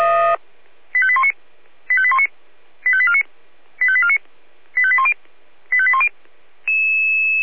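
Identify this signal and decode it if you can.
Five-tone selective call (ZVEI), heard as quick runs of five tones, tones of about 80 ms. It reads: 85819 85819 85849 85849 85810 85810
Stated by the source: ZVEI five-tone selective call 85870, received by SDR.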